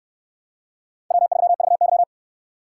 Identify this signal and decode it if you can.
Morse code, heard as a keyed tone, about 69 wpm, 700 Hz. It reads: W3VY